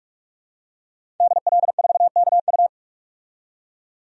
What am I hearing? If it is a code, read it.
Morse code, heard as a keyed tone, about 41 wpm, 700 Hz. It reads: DL4KU
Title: DXCC-100 Call Signs - 5 Characters - 45wpm